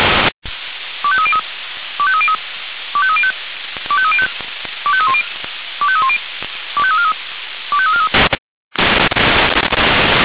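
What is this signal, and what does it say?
Five-tone selective call (ZVEI), heard as quick runs of five tones, tones of about 70 ms. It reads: 25303 25303 25305 25305 25310 25310 25343 25343
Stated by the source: Five-tone ZVEI selective call received by SDR.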